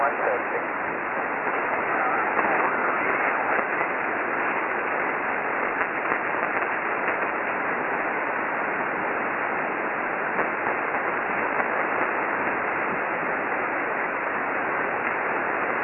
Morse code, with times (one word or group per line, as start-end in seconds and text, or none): none